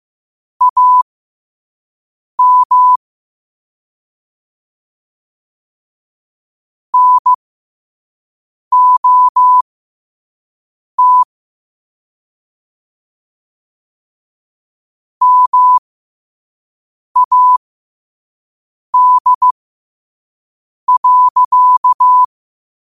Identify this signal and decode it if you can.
Morse code, heard as a keyed tone, about 15 wpm, spaced as for 5 wpm, 1000 Hz. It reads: AM NOT MAD.
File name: qotdfiles / 5wpm